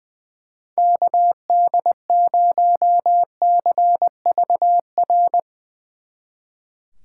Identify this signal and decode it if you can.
Morse code, heard as a keyed tone, about 22 wpm, 700 Hz. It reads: KD0CVR
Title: Speed Racing - USA 2x3 Call Signs 20wpm